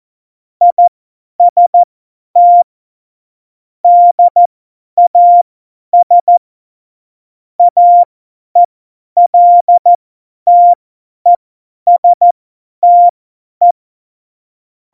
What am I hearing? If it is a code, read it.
Morse code, heard as a keyed tone, about 14 wpm, spaced as for 10 wpm, 700 Hz. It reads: IST DAS AELTESTE
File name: Morsekurs / Witze / Witz 26. November 2019, 50 F70